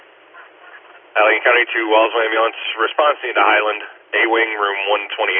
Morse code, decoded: none

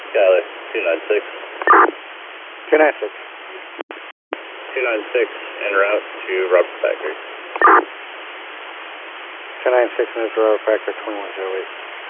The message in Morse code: none

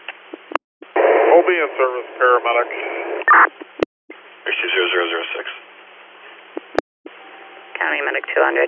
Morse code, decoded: none

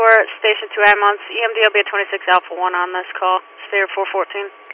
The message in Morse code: none

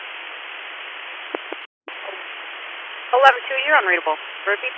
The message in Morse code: none

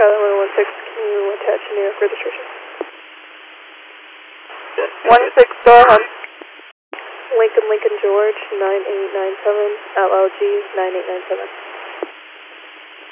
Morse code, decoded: none

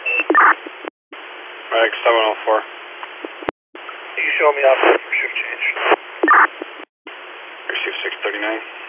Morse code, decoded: none

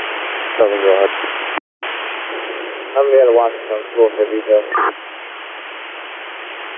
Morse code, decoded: none

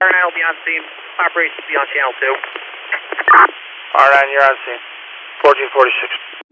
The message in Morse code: none

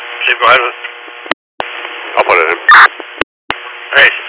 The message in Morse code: none

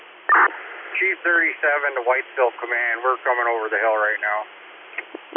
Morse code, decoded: none

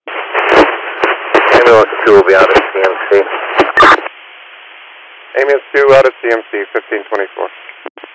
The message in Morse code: none

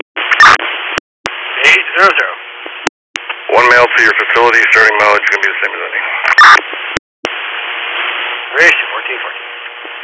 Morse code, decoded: none

